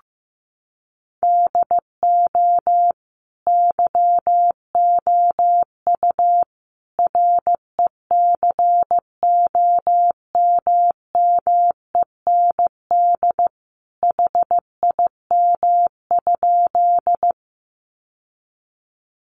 Morse code, heard as a keyed tone, about 15 wpm, 700 Hz. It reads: DO YOU RECOMMEND HIM?
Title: Sentences from Top 1200 Words 15wpm